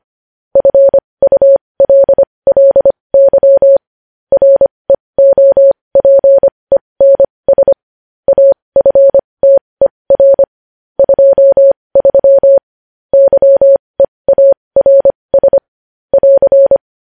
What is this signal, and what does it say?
Morse code, heard as a keyed tone, about 25 wpm, spaced as for 20 wpm, 550 Hz. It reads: FULLY REOPENS AFTER 23 YEARS <AR>
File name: podcast / News headlines-1x-25x20